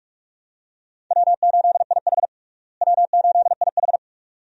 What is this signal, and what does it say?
Morse code, heard as a keyed tone, about 48 wpm, 700 Hz. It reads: W8IH W8IH
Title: USA Call Signs - 4 Characters - Repeated 2x - 45wpm